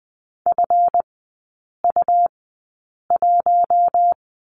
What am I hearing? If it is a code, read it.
Morse code, heard as a keyed tone, about 20 wpm, spaced as for 8 wpm, 700 Hz. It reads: FU1
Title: 152 Lesson 25e - Letter p - Three Characters 20wpm